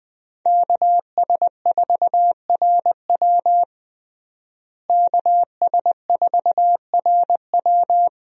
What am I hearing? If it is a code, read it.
Morse code, heard as a keyed tone, about 20 wpm, 700 Hz. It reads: KS4RW KS4RW
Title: USA Call Signs - 5 Characters - Repeated 3x - 20wpm